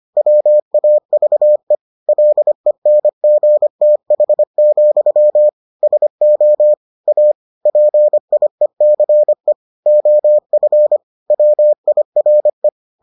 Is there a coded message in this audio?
Yes: Morse code from 0.2 to 12.7 s, heard as a keyed tone, about 25 wpm, 600 Hz. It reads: WAVE LENGTH, SO A PIECE OF WIRE